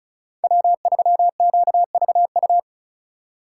Morse code, heard as a keyed tone, about 34 wpm, 700 Hz. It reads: W3QVU
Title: USA Call Signs - 5 Characters - 4x Character Spacing - 35wpm